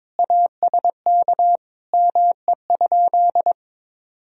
Morse code, heard as a keyed tone, about 22 wpm, 700 Hz. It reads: ASK ME?
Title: Sentences from Top 200 Words and Abbreviations 22wpm